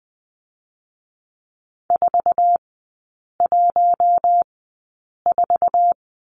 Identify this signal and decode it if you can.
Morse code, heard as a keyed tone, about 20 wpm, spaced as for 8 wpm, 700 Hz. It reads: 414
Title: 060 Lesson 11e - Letter h - Three Characters 20wpm